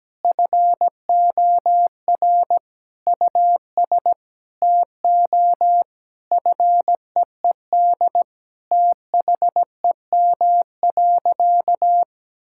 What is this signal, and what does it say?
Morse code, heard as a keyed tone, about 17 wpm, 700 Hz. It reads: FOR US TO FEED THEM.
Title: Sentences from Top 1600 Words 17wpm